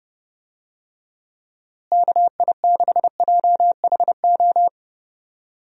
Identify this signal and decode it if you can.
Morse code, heard as a keyed tone, about 30 wpm, 700 Hz. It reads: KI6JHO
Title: USA Call Signs - 6 Characters - Repeated 2x - 30wpm